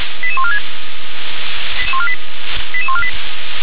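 Five-tone selective call (ZVEI), heard as quick runs of five tones, tones of about 70 ms. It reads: 88146 88148
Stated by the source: ZVEI five-tone selective call received by SDR.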